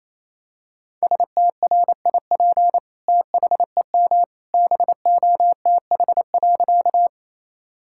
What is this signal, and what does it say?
Morse code, heard as a keyed tone, about 28 wpm, 700 Hz. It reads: STRIP THEM BOTH.
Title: Sentences from Top 1600 Words 28wpm